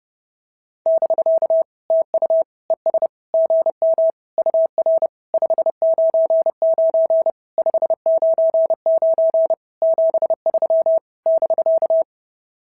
Morse code, heard as a keyed tone, about 30 wpm, 690 Hz. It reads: <BK> TU ES GM UR 599 599 73 <BK>
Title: SOTA QSO Practice 30wpm